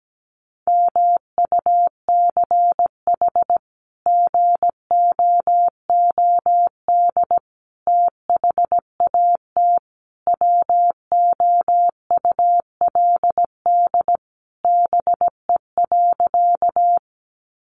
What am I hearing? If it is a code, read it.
Morse code, heard as a keyed tone, about 17 wpm, 700 Hz. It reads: MUCH GOOD THAT WOULD BE.
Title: Sentences from Top 200 Words 17wpm